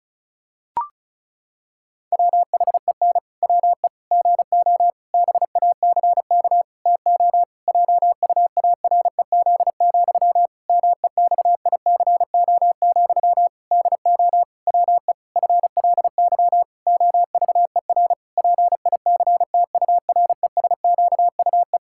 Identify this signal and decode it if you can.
Morse code, heard as a keyed tone, about 36 wpm, 700 Hz. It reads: WHEN WE GO BACK TO JUAREZ, MEXICO, DO WE FLY OVER PICTURESQUE
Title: Pangram Challenge v2 - 35wpm